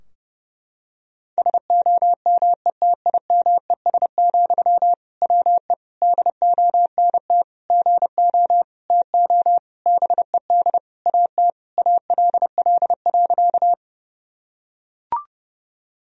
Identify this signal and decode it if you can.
Morse code, heard as a keyed tone, about 30 wpm, 700 Hz. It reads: SOMETIMES, WE DONT GO TO BED AT ALL.
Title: Sentences from Top 1000 Words - Review Part 3 - 30wpm